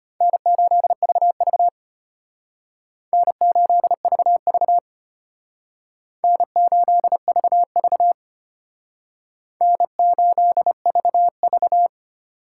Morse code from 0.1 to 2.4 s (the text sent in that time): N8VV